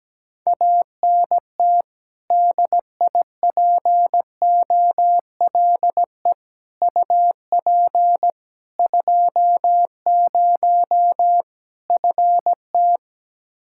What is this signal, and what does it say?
Morse code, heard as a keyed tone, about 17 wpm, 700 Hz. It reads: ANT DIPOLE UP 20 FT
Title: Practice QSO - Exchange 2 - Info - 17wpm